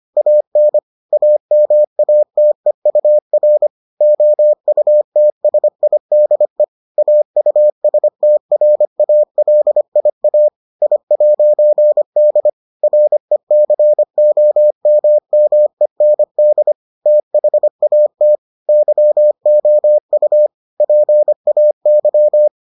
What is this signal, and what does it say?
Morse code, heard as a keyed tone, about 25 wpm, 600 Hz. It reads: AN AMATEUR OUTSIDE AUSTRALIA I'D RECOMMEND THAT YOU PAY